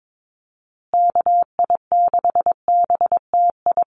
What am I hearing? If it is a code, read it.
Morse code, heard as a keyed tone, about 22 wpm, 700 Hz. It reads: KI6BTI
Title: USA Call Signs Repeated 2x 22wpm